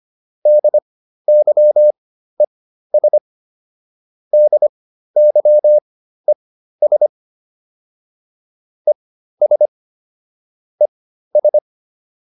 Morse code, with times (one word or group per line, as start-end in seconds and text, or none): 0.5-3.2 DYES
4.3-7.1 DYES
8.9-9.7 ES
10.8-11.6 ES